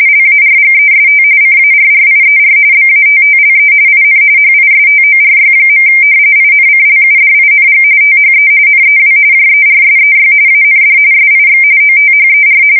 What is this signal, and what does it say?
RTTY, about 45.45 baud. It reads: LARGEST CARRIER BATTLE IN HISTORY. INCLUDING THE NEW JERSEY AND SIX OTHER
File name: fsk170 / battleshipnj